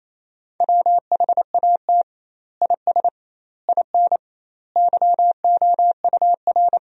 Morse code, heard as a keyed tone, about 28 wpm, 700 Hz. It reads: WHAT IS IN YOUR
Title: 2x Word-Spacing - Sentences from Top 200 Words - 28wpm